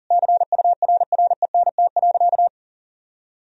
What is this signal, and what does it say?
Morse code, heard as a keyed tone, about 37 wpm, 700 Hz. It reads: CURRENT.
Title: Sentences from Top 1000 Words - Review Part 4 - 40wpm